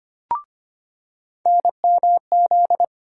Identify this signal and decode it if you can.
Morse code, heard as a keyed tone, about 25 wpm, 700 Hz. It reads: NMZ
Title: USA Call Sign Postfixes - 3 Letters 25wpm